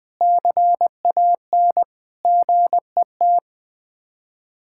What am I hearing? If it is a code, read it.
Morse code, heard as a keyed tone, about 20 wpm, 700 Hz. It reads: CAN GET
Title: Sets of 8 Words - 20wpm